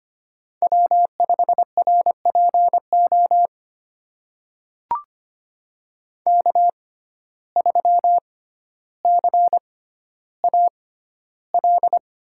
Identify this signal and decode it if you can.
Morse code, heard as a keyed tone, about 25 wpm, spaced as for 13 wpm, 700 Hz. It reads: W5RPO K3CAL
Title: USA Call Signs - 5 Characters - 6x Character Spacing - 25wpm